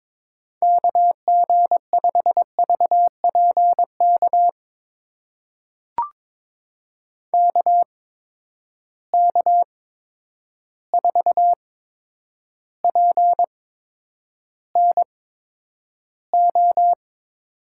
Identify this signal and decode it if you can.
Morse code, heard as a keyed tone, about 22 wpm, spaced as for 9 wpm, 700 Hz. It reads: KG5VPK KK4PNO
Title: USA Call Signs - 6 Characters - 8x Character Spacing - 22wpm